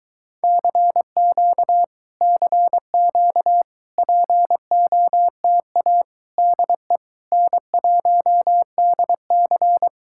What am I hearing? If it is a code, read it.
Morse code, heard as a keyed tone, about 23 wpm, 700 Hz. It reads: CQ CQ POTA DE N1DC